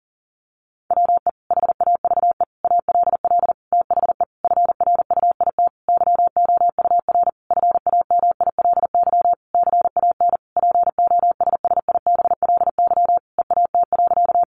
Morse code, heard as a keyed tone, about 40 wpm, 700 Hz. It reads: WE HAVE ALL THE FRUIT YOUR FAMILY CAN POSSIBLY EAT.
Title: Sentences from Top 1300 Words 40wpm